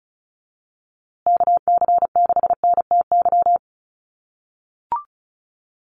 Morse code, heard as a keyed tone, about 34 wpm, 700 Hz. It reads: KC6NTY
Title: USA Call Signs Repeated 2x 35wpm